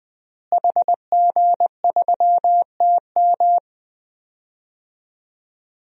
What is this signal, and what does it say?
Morse code, heard as a keyed tone, about 20 wpm, 700 Hz. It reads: HG3TM
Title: DXCC-100 Call Signs - 5 Characters - Repeated 2x - 20wpm